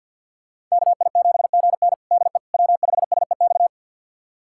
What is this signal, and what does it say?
Morse code, heard as a keyed tone, about 46 wpm, 700 Hz. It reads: KI7GN DE W5SEX
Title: USA Callsign DE Callsign - 5 Characters - 50wpm